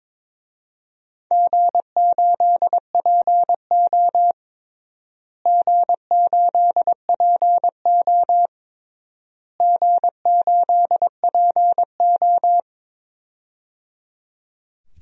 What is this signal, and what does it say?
Morse code, heard as a keyed tone, about 21 wpm, 700 Hz. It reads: G8PO G8PO G8PO
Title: DXCC-100 Call Signs - 4 Characters - Repeated 3x - 22wpm